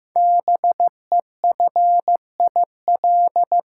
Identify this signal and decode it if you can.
Morse code, heard as a keyed tone, about 15 wpm, 700 Hz. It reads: BEFIL
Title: Code Groups - 5 Letters - 15wpm